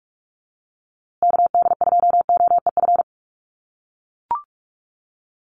Morse code, heard as a keyed tone, about 44 wpm, 700 Hz. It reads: KD2OEF
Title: USA Call Signs Repeated 3x 45wpm